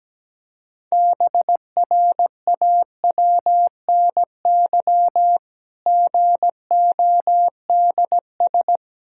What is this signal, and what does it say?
Morse code, heard as a keyed tone, about 17 wpm, 700 Hz. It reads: BRAWNY GODS
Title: Pangram Challenge v2 - 17wpm